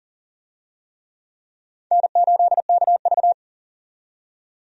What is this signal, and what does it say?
Morse code, heard as a keyed tone, about 40 wpm, 700 Hz. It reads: N8KV
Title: USA Call Signs - 4 Characters - 40wpm